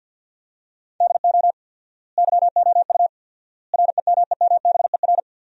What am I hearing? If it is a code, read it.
Morse code, heard as a keyed tone, about 51 wpm, 700 Hz. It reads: DO YOU REMEMBER
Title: 4x Word-Spacing - Sentences from Top 400 Words - 50wpm